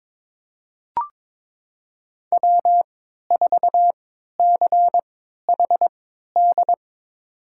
Morse code, heard as a keyed tone, about 22 wpm, spaced as for 13 wpm, 700 Hz. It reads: W4CHD
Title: USA Call Signs - 5 Characters - 3x Character Spacing - 22wpm